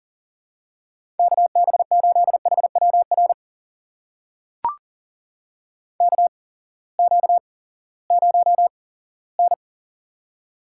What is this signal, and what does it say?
Morse code, heard as a keyed tone, about 38 wpm, spaced as for 20 wpm, 690 Hz. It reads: KB8HWR KQ0N
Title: USA Call Signs - 8x Character Spacing - 40wpm